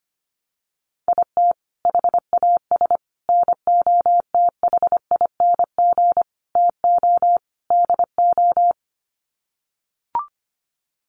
Morse code, heard as a keyed tone, about 25 wpm, 700 Hz. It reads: IT HAS NOTHING TO DO
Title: Sets of 5 Words Encore 25wpm